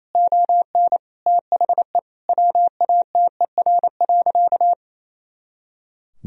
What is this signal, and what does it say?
Morse code, heard as a keyed tone, about 28 wpm, 700 Hz. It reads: ON THE WATER.
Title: Sentences from Top 500 Words and Abbreviations 28wpm